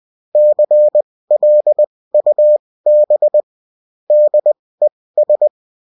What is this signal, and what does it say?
Morse code, heard as a keyed tone, about 20 wpm, spaced as for 16 wpm, 600 Hz. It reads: CLUB DES